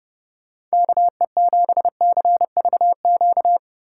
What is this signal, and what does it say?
Morse code, heard as a keyed tone, about 31 wpm, 700 Hz. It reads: KE7CVQ
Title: Speed Racing - USA 2x3 Call Signs 30wpm